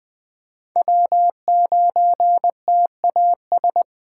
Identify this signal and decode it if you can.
Morse code, heard as a keyed tone, about 21 wpm, 700 Hz. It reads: W9TAS
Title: USA Call Signs - 5 Characters - 6x Character Spacing - 20wpm